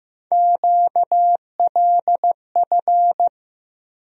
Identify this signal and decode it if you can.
Morse code, heard as a keyed tone, about 15 wpm, 700 Hz. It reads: QLF
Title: USA Call Sign Postfixes - 3 Letters 15wpm